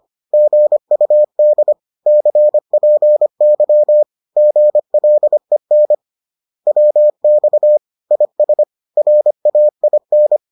Morse code, heard as a keyed tone, about 25 wpm, 600 Hz. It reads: GUD CPY GLEN WX IS RAIN